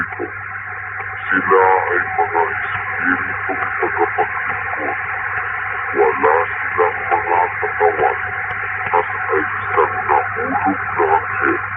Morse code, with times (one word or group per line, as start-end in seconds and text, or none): none